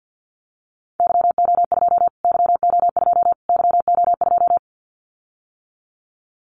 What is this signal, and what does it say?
Morse code, heard as a keyed tone, about 50 wpm, 700 Hz. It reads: YO2 YO2 YO2
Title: Callsign Prefixes for Top 25 DXCC - Repeated 3x - 50wpm